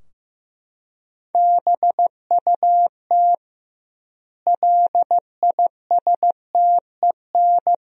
Morse code, heard as a keyed tone, about 15 wpm, 700 Hz. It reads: BUT LISTEN
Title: 2x Word-Spacing - Sentences from Top 400 Words - 15wpm